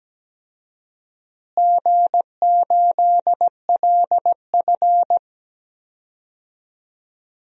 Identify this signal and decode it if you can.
Morse code, heard as a keyed tone, about 17 wpm, 700 Hz. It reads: G8LF
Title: DXCC-100 Call Signs - 4 Characters - Repeated 3x - 17wpm